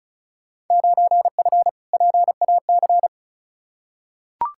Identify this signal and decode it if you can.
Morse code, heard as a keyed tone, about 36 wpm, 690 Hz. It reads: 9F PAC
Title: Field Day Exchanges Encore - 35wpm